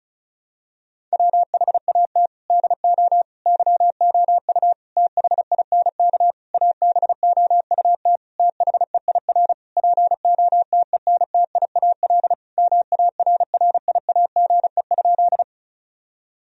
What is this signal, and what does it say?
Morse code, heard as a keyed tone, about 36 wpm, 700 Hz. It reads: WHAT DO YOU THINK ABOUT THEIR POTENTIAL MARRIAGE?
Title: Sentences from Top 1000 Words - Review Part 4 - 35wpm